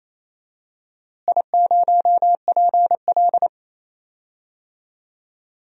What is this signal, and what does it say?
Morse code, heard as a keyed tone, about 28 wpm, 700 Hz. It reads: I0PL
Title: DXCC-100 Call Signs - 4 Characters - Repeated 2x - 28wpm